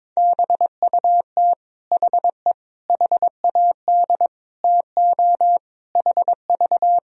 Morse code, heard as a keyed tone, about 22 wpm, 700 Hz. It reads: BUT HE HAD TO HV
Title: Sentences from Top 100 Words and Abbreviations 22wpm